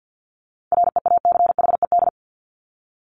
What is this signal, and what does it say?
Morse code, heard as a keyed tone, about 50 wpm, 700 Hz. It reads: REACHED
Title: ED Suffix - Top 100 Words with Base Words 50wpm